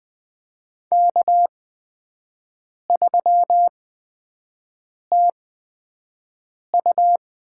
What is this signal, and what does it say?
Morse code, heard as a keyed tone, about 20 wpm, spaced as for 5 wpm, 700 Hz. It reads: K3TU
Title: USA Call Signs - 8x Character Spacing - 20wpm